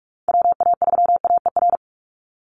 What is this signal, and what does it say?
Morse code, heard as a keyed tone, about 45 wpm, 700 Hz. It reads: WA3AER
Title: Call Signs - 45wpm